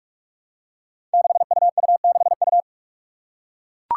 Morse code, heard as a keyed tone, about 41 wpm, 700 Hz. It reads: 6UU6U
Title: Five Challenging Letter-Number Combinations 45wpm